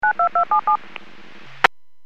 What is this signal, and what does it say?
DTMF, heard as telephone touch tones, tones of about 110 ms. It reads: 92277
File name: WPS / 01315 18-13 WPS 451.25 WPS